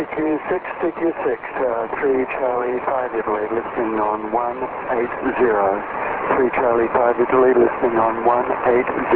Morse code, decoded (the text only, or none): none